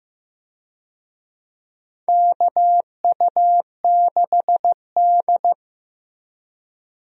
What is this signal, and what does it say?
Morse code, heard as a keyed tone, about 15 wpm, 700 Hz. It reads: KU6D
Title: USA Call Signs - 4 Characters - Repeated 3x - 15wpm